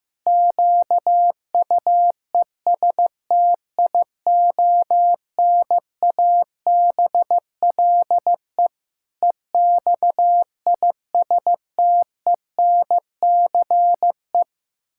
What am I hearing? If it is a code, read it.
Morse code, heard as a keyed tone, about 15 wpm, 700 Hz. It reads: QUESTIONABLE EXISTENCE
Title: Pangram Challenge v2 - 15wpm